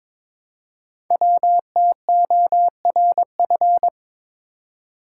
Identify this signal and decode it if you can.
Morse code, heard as a keyed tone, about 22 wpm, 700 Hz. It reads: WTORF
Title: Code Groups - 5 Letter:Numbers - 22wpm